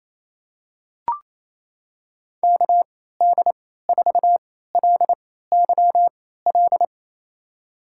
Morse code, heard as a keyed tone, about 28 wpm, spaced as for 16 wpm, 700 Hz. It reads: KD4LYL
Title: USA Call Signs - 6 Characters - 3x Character Spacing - 28wpm